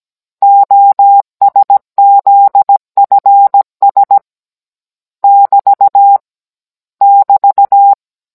Morse code, heard as a keyed tone, about 17 wpm, 800 Hz. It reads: OSZFS <BT> <BT>